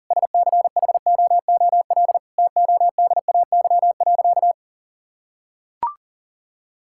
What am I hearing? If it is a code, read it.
Morse code, heard as a keyed tone, about 40 wpm, 700 Hz. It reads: SCHOOL TODAY.
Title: Sentences from Top 1000 Words - Review Part 5 - 40wpm